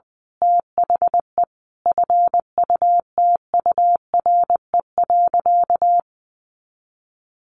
Morse code, heard as a keyed tone, about 19 wpm, 700 Hz. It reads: THE FUTURE.
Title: A Sound Of Thunder 20wpm